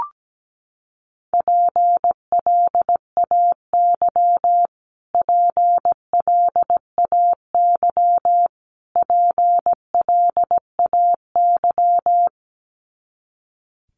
Morse code, heard as a keyed tone, about 17 wpm, 700 Hz. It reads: PLAY PLAY PLAY
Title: Top 200 Words Repeated 17wpm